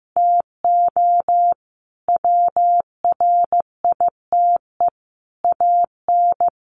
Morse code, heard as a keyed tone, about 15 wpm, 700 Hz. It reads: TO WRITE AN
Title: Sentences from Top 1500 Words 15wpm